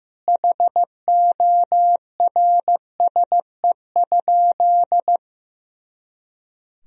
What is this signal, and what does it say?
Morse code, heard as a keyed tone, about 15 wpm, 700 Hz. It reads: HORSE?
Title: Sentences from Top 500 Words and Abbreviations 15wpm